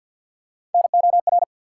Morse code, heard as a keyed tone, about 53 wpm, 700 Hz. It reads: NOR
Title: Sentences from Top 1500 Words - New Word Drill - 50wpm